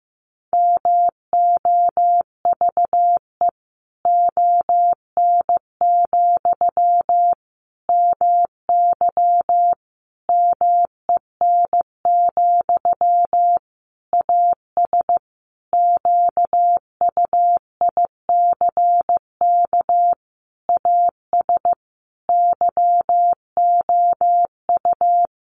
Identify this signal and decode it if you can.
Morse code, heard as a keyed tone, about 15 wpm, 700 Hz. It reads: MOVE ON, MY MEN, AS QUICK AS YOU
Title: Sentences from Top 500 Words 15wpm